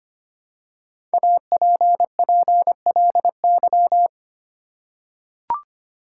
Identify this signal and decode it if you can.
Morse code, heard as a keyed tone, about 25 wpm, 700 Hz. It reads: APPLY
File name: 5 Letter Words - First 3 Letters Given - 25wpm